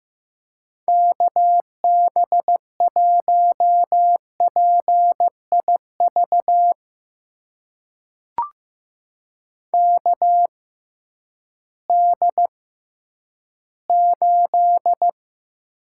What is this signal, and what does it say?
Morse code, heard as a keyed tone, about 16 wpm, spaced as for 10 wpm, 700 Hz. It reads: KB1PIV KD8
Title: USA Call Signs - 6 Characters - 6x Character Spacing - 15wpm